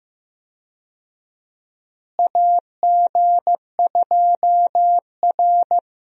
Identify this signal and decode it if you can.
Morse code, heard as a keyed tone, about 15 wpm, 700 Hz. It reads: AG2R
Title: USA Call Signs - 4 Characters - Repeated 3x - 15wpm